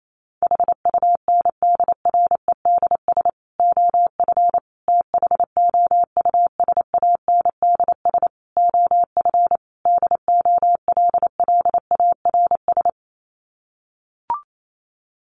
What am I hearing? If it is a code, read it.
Morse code, heard as a keyed tone, about 28 wpm, 700 Hz. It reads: HUNDREDS OF THOUSANDS OF DOLLARS
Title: Sets of 5 Words Encore 28wpm